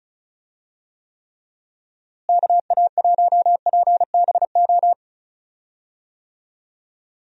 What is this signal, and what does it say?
Morse code, heard as a keyed tone, about 34 wpm, 700 Hz. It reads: KA1PBO
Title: USA Call Signs - 6 Characters - Repeated 2x - 35wpm